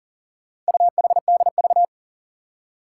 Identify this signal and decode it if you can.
Morse code, heard as a keyed tone, about 41 wpm, 700 Hz. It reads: UHDV
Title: Four Challenging Letter-Number Combinations 40wpm